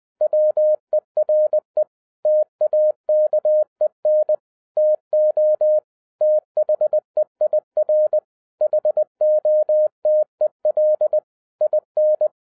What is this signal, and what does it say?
Morse code, heard as a keyed tone, about 20 wpm, 610 Hz. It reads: WERE TAKEN TO THEIR HOTEL IN